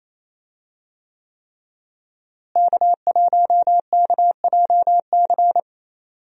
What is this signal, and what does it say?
Morse code, heard as a keyed tone, about 28 wpm, 700 Hz. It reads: K1KJC
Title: USA Call Signs - 5 Characters - Repeated 3x - 28wpm